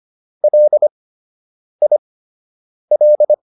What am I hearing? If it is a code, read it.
Morse code, heard as a keyed tone, about 25 wpm, spaced as for 8 wpm, 600 Hz. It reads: LIL